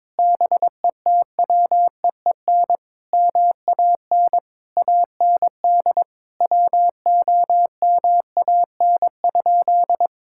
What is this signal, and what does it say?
Morse code, heard as a keyed tone, about 22 wpm, 700 Hz. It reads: BETWEEN MAN AND WOMAN?
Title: Sentences from Top 1600 Words 22wpm